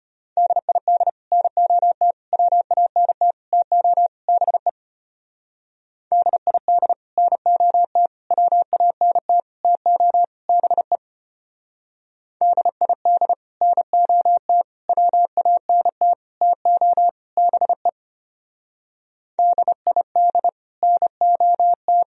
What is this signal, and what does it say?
Morse code, heard as a keyed tone, about 31 wpm, 700 Hz. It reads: DID NOT WANT TO BE DID NOT WANT TO BE DID NOT WANT TO BE DID NOT